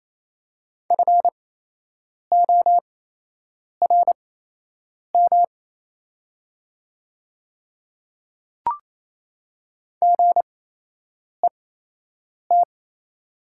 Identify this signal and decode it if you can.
Morse code, heard as a keyed tone, about 28 wpm, spaced as for 8 wpm, 700 Hz. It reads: FORM GET
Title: 8x Character Spacing - Top 200 Words - 28wpm